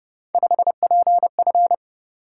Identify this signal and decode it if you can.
Morse code, heard as a keyed tone, about 29 wpm, 700 Hz. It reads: 5PF
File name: Speed-Racing - Three Letter-Number Combinations - 25wpm